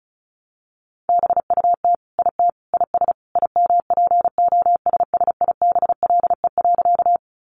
Morse code, heard as a keyed tone, about 34 wpm, 700 Hz. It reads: BUT IT IS IMPOSSIBLE.
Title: Sentences from Top 1100 Words 35wpm